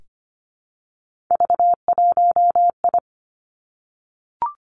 Morse code, heard as a keyed tone, about 25 wpm, 700 Hz. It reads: V1I
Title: Three Challenging Letter-Number Combinations 25wpm